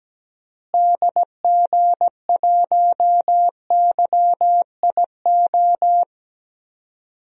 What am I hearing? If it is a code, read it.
Morse code, heard as a keyed tone, about 17 wpm, 700 Hz. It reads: DG1YIO
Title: DXCC-100 Call Signs - 6 Characters - Repeated 2x - 17wpm